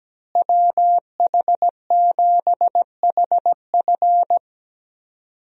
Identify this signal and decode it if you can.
Morse code, heard as a keyed tone, about 17 wpm, 700 Hz. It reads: WH7HF
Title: USA Call Signs - 5 Characters - Repeated 3x - 17wpm